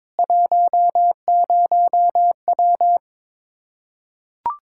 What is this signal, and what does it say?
Morse code, heard as a keyed tone, about 22 wpm, 700 Hz. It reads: 10W
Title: Practice QSO - Exchange 2 - Info - 22wpm